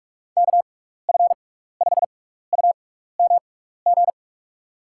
Morse code, heard as a keyed tone, about 44 wpm, spaced as for 15 wpm, 700 Hz. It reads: KF5UMG
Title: USA Call Signs - 6 Characters - 6x Character Spacing - 45wpm